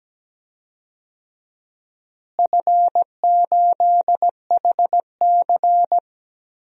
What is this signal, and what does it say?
Morse code, heard as a keyed tone, about 17 wpm, 700 Hz. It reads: F8HC
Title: DXCC-100 Call Signs - 4 Characters - Repeated 3x - 17wpm